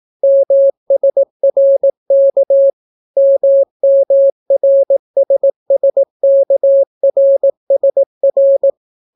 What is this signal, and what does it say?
Morse code, heard as a keyed tone, about 18 wpm, 550 Hz. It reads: MSRK MMRSSKRSR